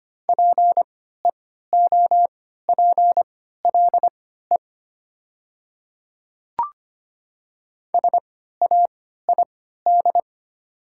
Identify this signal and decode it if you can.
Morse code, heard as a keyed tone, about 25 wpm, spaced as for 14 wpm, 700 Hz. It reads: PEOPLE SAID
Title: 3x Character Spacing - Top 100 Words - 25wpm